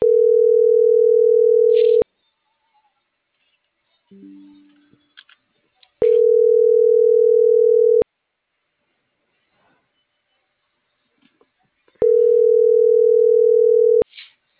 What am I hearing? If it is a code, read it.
Ringback tone (North America), 440 + 480 Hz.